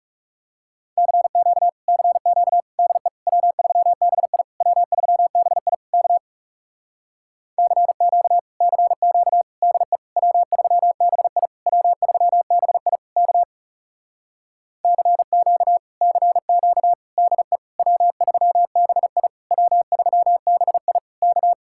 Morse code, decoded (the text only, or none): CQ CQ DE W3BI W3BI K CQ CQ DE W3BI W3BI K CQ CQ DE W3BI W3BI K